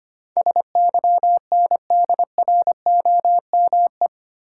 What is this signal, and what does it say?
Morse code, heard as a keyed tone, about 25 wpm, 700 Hz. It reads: SYNDROME
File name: Eight Letter Words - 25wpm